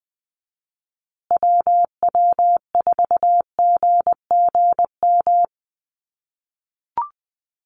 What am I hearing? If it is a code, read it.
Morse code, heard as a keyed tone, about 20 wpm, 700 Hz. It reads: WW4GGM
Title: USA Call Signs Repeated 2x 20wpm